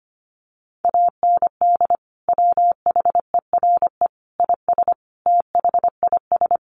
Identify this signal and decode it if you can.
Morse code, heard as a keyed tone, about 25 wpm, 700 Hz. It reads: AND WHERE IS THIS
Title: Sentences from Top 200 Words 25wpm